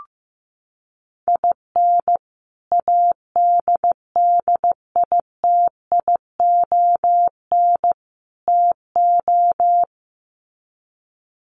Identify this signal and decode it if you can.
Morse code, heard as a keyed tone, about 15 wpm, 700 Hz. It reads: IN ADDITION TO
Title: Sets of 3 Words 15wpm